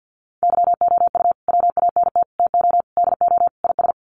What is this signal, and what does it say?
Morse code, heard as a keyed tone, about 48 wpm, 700 Hz. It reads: YOU WANT TO DO IS
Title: Sets of 5 Words Encore2 50wpm